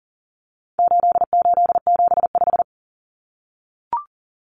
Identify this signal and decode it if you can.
Morse code, heard as a keyed tone, about 40 wpm, 700 Hz. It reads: 8875
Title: Common Cut Numbers - 4-Digit 40wpm